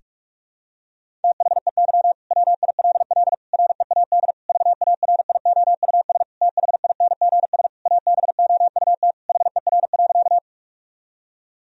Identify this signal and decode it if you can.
Morse code, heard as a keyed tone, about 46 wpm, 700 Hz. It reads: THEY WILL READ VARIOUS THINGS ABOUT HER.
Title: Sentences from Top 1000 Words - Review Part 4 - 45wpm